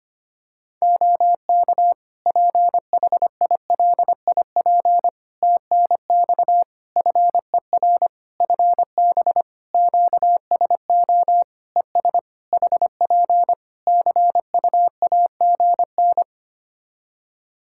Morse code, heard as a keyed tone, about 25 wpm, 700 Hz. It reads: OK PHILIP TNX FER FB QSO ES HP CUAGN
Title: K1USN SST QSO Practice 25wpm